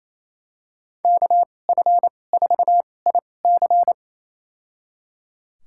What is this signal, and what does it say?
Morse code, heard as a keyed tone, about 28 wpm, spaced as for 20 wpm, 700 Hz. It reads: KF4IC
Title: USA Call Signs - 5 Characters - 2x Character Spacing - 28wpm